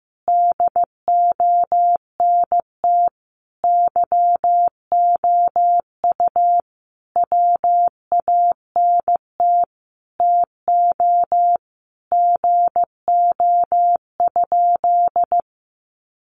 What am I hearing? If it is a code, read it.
Morse code, heard as a keyed tone, about 15 wpm, 700 Hz. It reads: DONT YOU WANT TO GO?
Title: Sentences from Top 300 Words 15wpm